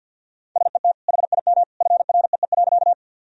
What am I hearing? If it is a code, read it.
Morse code, heard as a keyed tone, about 50 wpm, 700 Hz. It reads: SET HIM FREE.
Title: Sentences from Top 1000 Words - Review Part 1 - 50wpm